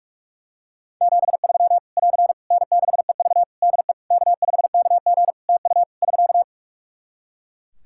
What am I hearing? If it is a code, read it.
Morse code, heard as a keyed tone, about 42 wpm, 700 Hz. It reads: 73 <AR> N6EV DE K5KG TU <SK>
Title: K1USN SST QSO Practice 45wpm